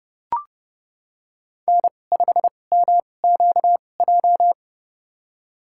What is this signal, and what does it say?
Morse code, heard as a keyed tone, about 30 wpm, spaced as for 22 wpm, 700 Hz. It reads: N5MQJ
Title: USA Call Signs - 5 Characters - 2x Character Spacing - 30wpm